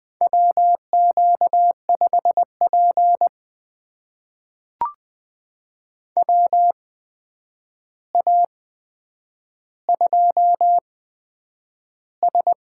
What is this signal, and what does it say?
Morse code, heard as a keyed tone, about 20 wpm, spaced as for 9 wpm, 700 Hz. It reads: WQ5P WA2S
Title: USA Call Signs - 4 Characters - 8x Character Spacing - 20wpm